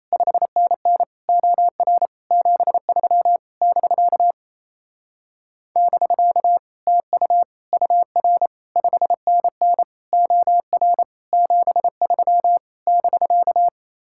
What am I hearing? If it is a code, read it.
Morse code, heard as a keyed tone, about 33 wpm, 700 Hz. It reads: 5NN OR 73 <BK> <BK> TU UR 5NN OR 73 <BK>
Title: Speed-Racing Common QSO Phrases Encore - 28wpm